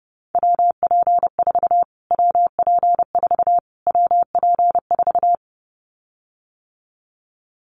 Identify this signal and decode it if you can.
Morse code, heard as a keyed tone, about 30 wpm, 700 Hz. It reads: WP4 WP4 WP4
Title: Callsign Prefixes for Top 25 DXCC - Repeated 3x - 30wpm